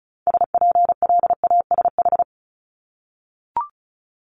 Morse code, heard as a keyed tone, about 36 wpm, 700 Hz. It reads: SPLASH
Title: Multi-letter Phonemes 35wpm